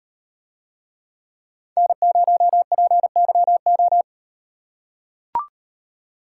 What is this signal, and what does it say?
Morse code, heard as a keyed tone, about 38 wpm, 700 Hz. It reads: N0PYO